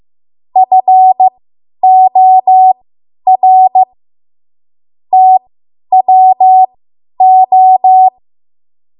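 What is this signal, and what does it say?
Morse code, heard as a keyed tone, about 15 wpm, spaced as for 10 wpm, 750 Hz. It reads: FOR TWO
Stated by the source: Morse code practice run at 10 WPM from a code practice archive.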